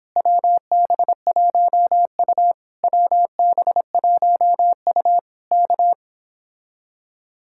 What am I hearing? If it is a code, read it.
Morse code, heard as a keyed tone, about 26 wpm, 700 Hz. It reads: WB1U WB1U K